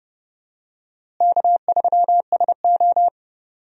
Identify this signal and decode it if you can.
Morse code, heard as a keyed tone, about 30 wpm, 700 Hz. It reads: K3SO
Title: USA Call Signs - 4 Characters - Repeated 3x - 30wpm